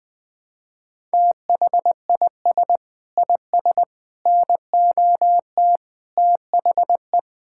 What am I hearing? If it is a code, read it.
Morse code, heard as a keyed tone, about 21 wpm, 700 Hz. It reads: THIS IS NOT THE
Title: Sets of 8 Words - 20wpm